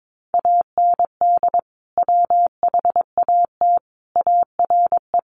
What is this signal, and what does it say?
Morse code, heard as a keyed tone, about 22 wpm, 700 Hz. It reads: AND WHAT ARE
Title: Sentences from Top 800 Words 22wpm